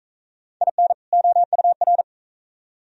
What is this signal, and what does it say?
Morse code, heard as a keyed tone, about 39 wpm, 700 Hz. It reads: IN OUR